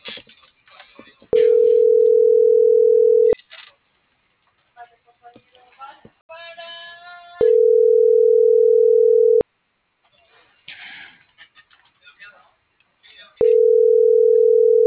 Ringback tone (North America), 440 + 480 Hz.